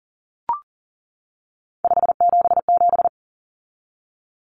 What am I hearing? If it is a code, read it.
Morse code, heard as a keyed tone, about 40 wpm, 700 Hz. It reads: 577
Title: Numbers - 3-Digit 40wpm